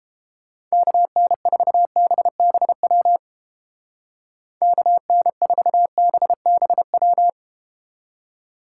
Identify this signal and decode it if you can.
Morse code, heard as a keyed tone, about 31 wpm, 700 Hz. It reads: KN4BBW KN4BBW